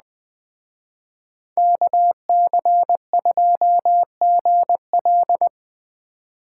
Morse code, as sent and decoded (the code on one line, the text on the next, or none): -.- -.-. ..--- --. .-..
KC2GL